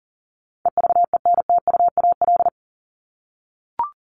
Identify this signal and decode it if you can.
Morse code, heard as a keyed tone, about 40 wpm, 700 Hz. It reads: EVENTUAL
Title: LY Suffix - Top 100 Words with Base Words 40wpm